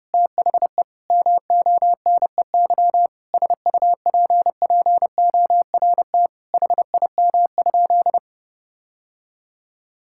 Morse code, heard as a keyed tone, about 32 wpm, 700 Hz. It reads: THE MONEY SUPPORT HIM?
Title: Sentences from Top 1000 Words - Review Part 5 - 30wpm